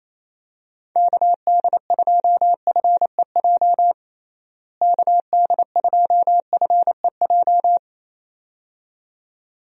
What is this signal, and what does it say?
Morse code, heard as a keyed tone, about 28 wpm, 700 Hz. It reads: KD2FEJ KD2FEJ
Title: USA Call Signs - 6 Characters - Repeated 2x - 28wpm